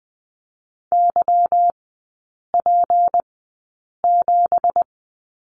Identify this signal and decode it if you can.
Morse code, heard as a keyed tone, about 20 wpm, spaced as for 8 wpm, 700 Hz. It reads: YP7
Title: 166 Lesson 27e - Number 7 - Three Characters 20wpm